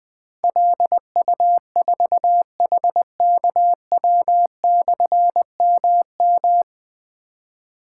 Morse code, heard as a keyed tone, about 20 wpm, 700 Hz. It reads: LU4HKW/MM
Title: Difficult Call Signs 20wpm